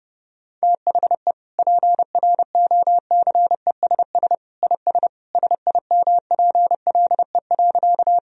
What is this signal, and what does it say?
Morse code, heard as a keyed tone, about 32 wpm, 700 Hz. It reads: THE PROCESS IS SIMPLE.
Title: Sentences from Top 1000 Words - Review Part 3 - 30wpm